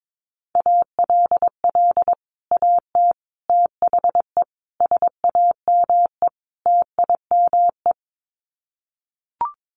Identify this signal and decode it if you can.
Morse code, heard as a keyed tone, about 22 wpm, 700 Hz. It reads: ALL AT THE SAME TIME
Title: Sets of 5 Words Encore2 22wpm